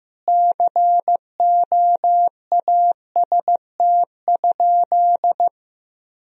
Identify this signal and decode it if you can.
Morse code, heard as a keyed tone, about 16 wpm, 700 Hz. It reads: COAST?
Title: Sentences from Top 1000 Words - Review Part 4 - 15wpm